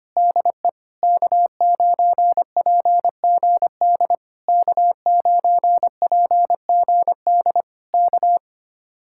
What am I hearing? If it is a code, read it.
Morse code, heard as a keyed tone, about 25 wpm, 700 Hz. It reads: DE K9PGD K9PGD K
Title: Speed-Racing Common QSO Phrases Encore - 25wpm